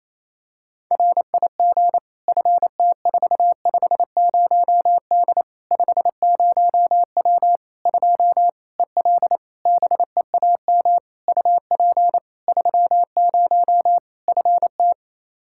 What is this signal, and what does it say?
Morse code, heard as a keyed tone, about 28 wpm, 700 Hz. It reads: RIG FT450D 50W 2 EL BEAM UP 30 FT
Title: Practice QSO - Exchange 2 - Info - 28wpm